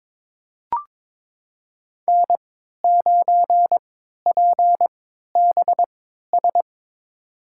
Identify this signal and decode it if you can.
Morse code, heard as a keyed tone, about 22 wpm, spaced as for 12 wpm, 700 Hz. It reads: N9PBS
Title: USA Call Signs - 5 Characters - 3x Character Spacing - 22wpm